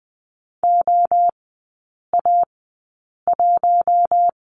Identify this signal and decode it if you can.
Morse code, heard as a keyed tone, about 20 wpm, spaced as for 8 wpm, 700 Hz. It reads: OA1
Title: 041 Lesson 8e - Number 1 - Three Characters 20wpm